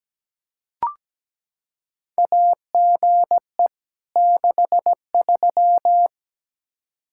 Morse code, heard as a keyed tone, about 17 wpm, 700 Hz. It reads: AGE 63
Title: K1USN SST QSO Practice 17wpm